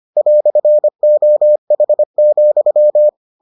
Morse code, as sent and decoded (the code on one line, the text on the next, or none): .-..-. --- .... --..--
"OH,